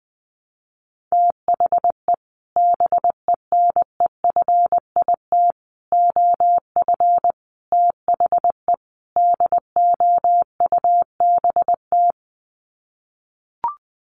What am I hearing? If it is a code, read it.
Morse code, heard as a keyed tone, about 20 wpm, 700 Hz. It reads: THE BENEFIT OF THE DOUBT
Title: Sets of 5 Words Encore 20wpm